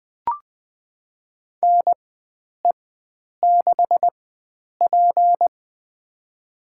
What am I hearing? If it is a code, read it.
Morse code, heard as a keyed tone, about 20 wpm, spaced as for 9 wpm, 700 Hz. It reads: NE6P
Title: USA Call Signs - 4 Characters - 4x Character Spacing - 20wpm